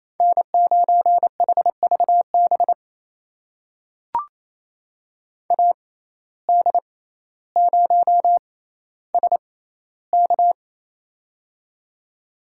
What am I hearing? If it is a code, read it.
Morse code, heard as a keyed tone, about 28 wpm, spaced as for 14 wpm, 700 Hz. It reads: N9HVB AD0SK
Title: USA Call Signs - 5 Characters - 6x Character Spacing - 28wpm